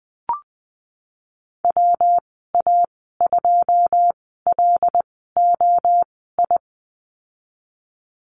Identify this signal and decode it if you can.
Morse code, heard as a keyed tone, about 19 wpm, spaced as for 14 wpm, 700 Hz. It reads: WA2LOI
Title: 147 Lesson 24g - Letter y - Call Signs 20wpm